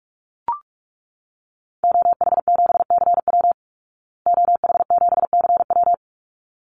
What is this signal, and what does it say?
Morse code, heard as a keyed tone, about 44 wpm, 700 Hz. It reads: OH7CW OH7CW
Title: Call Signs Repeated 3x - 45wpm